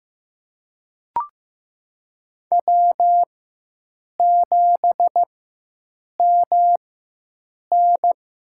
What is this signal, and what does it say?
Morse code, heard as a keyed tone, about 15 wpm, spaced as for 7 wpm, 700 Hz. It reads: W7MN
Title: USA Call Signs - 4x Character Spacing - 15wpm